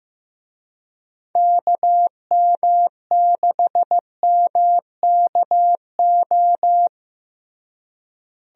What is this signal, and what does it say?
Morse code, heard as a keyed tone, about 16 wpm, 700 Hz. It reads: KM6MKO
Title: USA Call Signs - 6 Characters - Repeated 3x - 15wpm